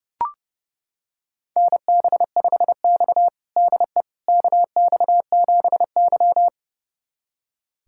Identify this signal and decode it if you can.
Morse code, heard as a keyed tone, about 32 wpm, 700 Hz. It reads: NB5X DE KX7Y
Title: USA Callsign DE Callsign - 4 Characters - 30wpm